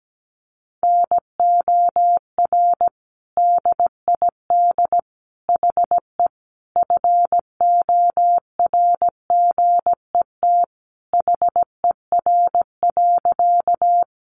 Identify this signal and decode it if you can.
Morse code, heard as a keyed tone, about 17 wpm, 700 Hz. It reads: NOR DID HE FORGET HER.
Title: Sentences from Top 1500 Words 17wpm